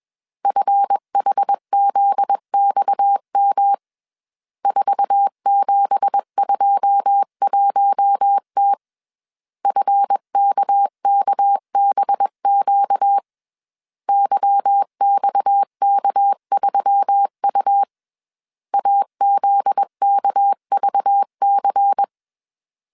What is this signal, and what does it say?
Morse code, heard as a keyed tone, about 21 wpm, 800 Hz. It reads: FHZXM 4721T FKKBQ YXK3U AZKVC
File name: CW / DET / 1 9 1500C S C 20wpm 2310